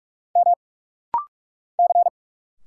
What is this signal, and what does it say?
Morse code, heard as a keyed tone, about 46 wpm, 690 Hz. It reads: M C
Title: Single Letters - Rapid-Fire 45wpm